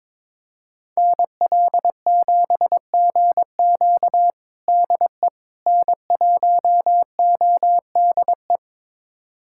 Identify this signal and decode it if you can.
Morse code, heard as a keyed tone, about 22 wpm, 700 Hz. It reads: NL7GQ DE N1ODE
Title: USA Callsign DE Callsign - 5 Characters - 22wpm